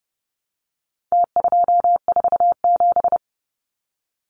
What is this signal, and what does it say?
Morse code, heard as a keyed tone, about 30 wpm, 700 Hz. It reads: T247
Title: Numbers - Contest Style - 4-Digit 30wpm